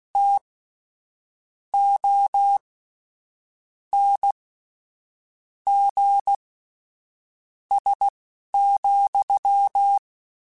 Morse code, heard as a keyed tone, about 16 wpm, spaced as for 6 wpm, 800 Hz. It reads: TONGS,
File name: Misc / Misc-mp3s / WordPairs sp @10